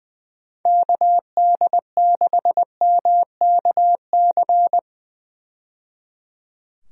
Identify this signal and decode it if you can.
Morse code, heard as a keyed tone, about 20 wpm, 700 Hz. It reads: KD6MKC
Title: USA Call Signs - 6 Characters - 20wpm